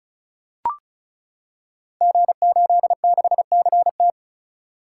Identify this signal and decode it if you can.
Morse code, heard as a keyed tone, about 36 wpm, 700 Hz. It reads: G86CT
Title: Code Groups - 5 Letter:Numbers - 35wpm